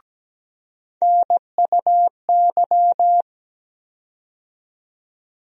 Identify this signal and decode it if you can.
Morse code, heard as a keyed tone, about 17 wpm, 700 Hz. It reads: NUY
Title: USA Call Sign Postfixes - 3 Letters 17wpm